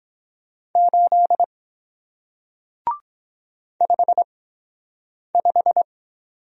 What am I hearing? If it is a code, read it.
Morse code, heard as a keyed tone, about 23 wpm, spaced as for 7 wpm, 700 Hz. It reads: 8 55